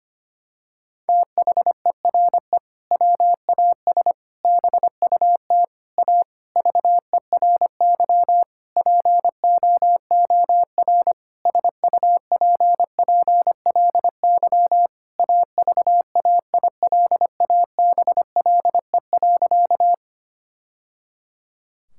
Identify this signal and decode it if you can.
Morse code, heard as a keyed tone, about 25 wpm, 700 Hz. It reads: THERE WAS BUT A VERY POOR SUPPLY AVAILABLE.
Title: Sentences from Top 1000 Words - Review Part 5 - 25wpm